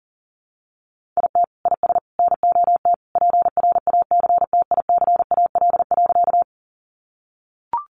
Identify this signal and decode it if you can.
Morse code, heard as a keyed tone, about 37 wpm, 700 Hz. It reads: IT IS NOT PRACTICAL.
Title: Sentences from Top 1400 Words 40wpm